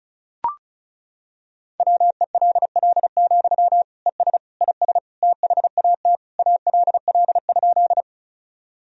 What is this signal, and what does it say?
Morse code, heard as a keyed tone, about 36 wpm, 700 Hz. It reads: WELL, ES IS THAT ALL?
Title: Sentences from Top 200 Words and Abbreviations 35wpm